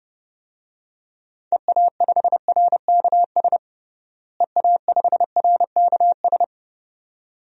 Morse code, heard as a keyed tone, about 30 wpm, 700 Hz. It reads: EA5RKS EA5RKS
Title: DXCC-100 Call Signs - 6 Characters - Repeated 2x - 30wpm